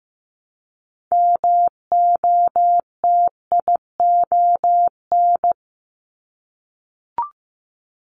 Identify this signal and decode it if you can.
Morse code, heard as a keyed tone, about 15 wpm, 700 Hz. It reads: MOTION
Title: Multi-letter Phonemes Repeated 3x 15wpm